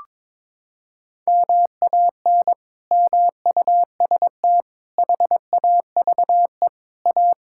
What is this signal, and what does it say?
Morse code, heard as a keyed tone, about 22 wpm, 700 Hz. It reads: MAN MUST HAVE A
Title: Sentences from Top 1000 Words - Review Part 6 - 22wpm